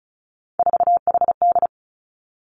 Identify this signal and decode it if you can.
Morse code, heard as a keyed tone, about 36 wpm, 700 Hz. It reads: 4HD
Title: Three Challenging Letter-Number Combinations 35wpm